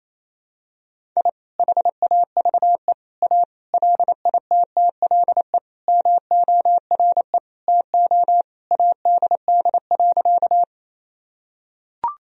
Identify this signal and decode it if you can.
Morse code, heard as a keyed tone, about 28 wpm, 700 Hz. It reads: I HAVE A LITTLE MORE TO ADD.
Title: Sentences from Top 1000 Words - Review Part 1 - 28wpm